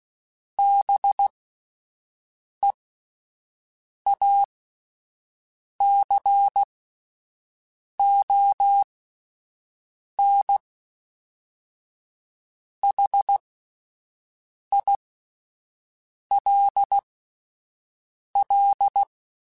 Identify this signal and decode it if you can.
Morse code, heard as a keyed tone, about 16 wpm, spaced as for 5 wpm, 800 Hz. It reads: BEACON HILL
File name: fromSMs / FromSMs mp3s / Cities@5